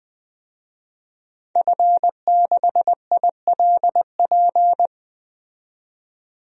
Morse code, heard as a keyed tone, about 20 wpm, 700 Hz. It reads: F6ILP
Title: DXCC-100 Call Signs - 5 Characters - 20wpm